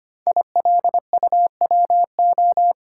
Morse code, heard as a keyed tone, about 25 wpm, 700 Hz. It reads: ILUWO
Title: Code Groups - 5 Letters - 25wpm